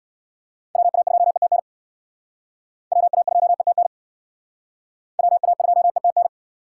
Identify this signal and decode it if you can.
Morse code, heard as a keyed tone, about 64 wpm, 700 Hz. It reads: WA2ETN WA2ETN WA2ETN